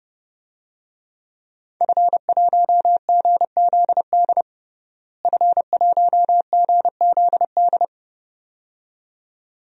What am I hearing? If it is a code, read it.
Morse code, heard as a keyed tone, about 32 wpm, 700 Hz. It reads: F1GZD F1GZD
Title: DXCC-100 Call Signs - 5 Characters - Repeated 2x - 30wpm